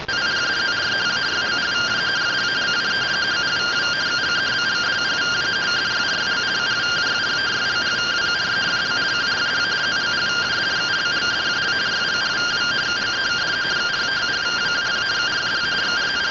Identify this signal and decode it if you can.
Two-tone teleprinter signal, about 170 Hz shift.